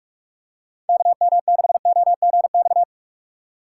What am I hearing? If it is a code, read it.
Morse code, heard as a keyed tone, about 46 wpm, 700 Hz. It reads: KM6OGX
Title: USA Call Signs - 6 Characters - 2x Character Spacing - 45wpm